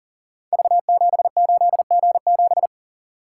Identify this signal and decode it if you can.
Morse code, heard as a keyed tone, about 40 wpm, 700 Hz. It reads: V78G7
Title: Code Groups - 5 Letter:Numbers - 40wpm